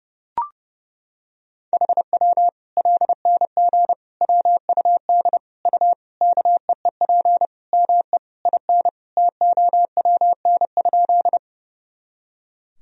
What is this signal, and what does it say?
Morse code, heard as a keyed tone, about 32 wpm, 700 Hz. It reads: HW LNG WUD U KEEP ME IN TOWN?
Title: Sentences from Top 500 Words and Abbreviations 30wpm